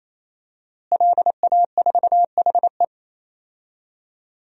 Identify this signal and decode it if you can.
Morse code, heard as a keyed tone, about 28 wpm, 700 Hz. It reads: LA4HE
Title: DXCC-100 Call Signs - 5 Characters - Repeated 3x - 28wpm